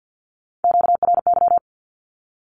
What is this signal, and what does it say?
Morse code, heard as a keyed tone, about 50 wpm, 700 Hz. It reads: QRY
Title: Q-Codes with Spoken Meaning - 50wpm